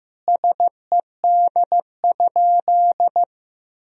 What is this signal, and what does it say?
Morse code, heard as a keyed tone, about 15 wpm, 700 Hz. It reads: SED?
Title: Sentences from Top 100 Words and Abbreviations 15wpm